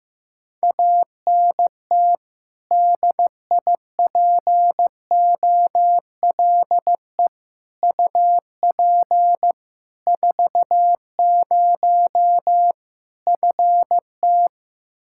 Morse code, heard as a keyed tone, about 15 wpm, 700 Hz. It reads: ANT DIPOLE UP 40 FT
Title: Practice QSO - Exchange 2 - Info - 15wpm